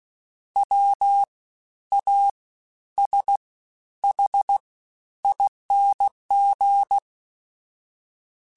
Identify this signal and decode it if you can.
Morse code, heard as a keyed tone, about 16 wpm, spaced as for 11 wpm, 800 Hz. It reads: WASHING
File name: Special INGs-mp3s / Special INGs-mp3s / Fast ing @15